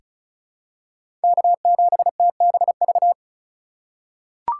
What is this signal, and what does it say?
Morse code, heard as a keyed tone, about 36 wpm, 700 Hz. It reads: K7TBV
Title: USA Call Signs - 5 Characters - 2x Character Spacing - 35wpm